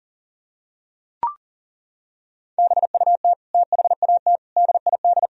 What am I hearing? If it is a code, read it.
Morse code, heard as a keyed tone, about 40 wpm, 700 Hz. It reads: BUT THAT DID
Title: Sentences from Top 1000 Words - Review Part 1 - 40wpm